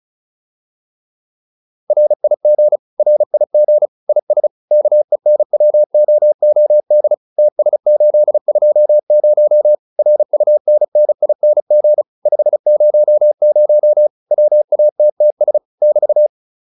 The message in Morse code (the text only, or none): RIG RIG IS KENWOOD TS820 RUNNING 500 WATTS <BT>